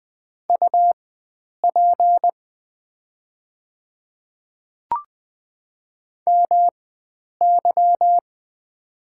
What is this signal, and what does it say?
Morse code, heard as a keyed tone, about 20 wpm, spaced as for 9 wpm, 700 Hz. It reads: UP MY
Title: 4x Character Spacing - Top 300 Words - 20wpm